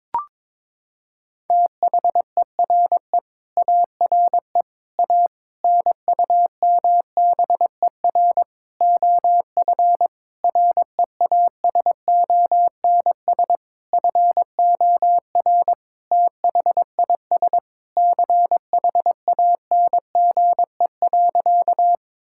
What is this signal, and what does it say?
Morse code, heard as a keyed tone, about 22 wpm, 700 Hz. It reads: THERE ARE A NUMBER OF REASONS FOR THIS CHANGE.
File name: practice / 22wpm / Sentences with 8-Word Phrases - 22wpm